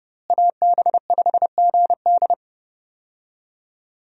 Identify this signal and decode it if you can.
Morse code, heard as a keyed tone, about 29 wpm, 700 Hz. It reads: AB5GD
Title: USA Call Signs - 5 Characters - Repeated 2x - 30wpm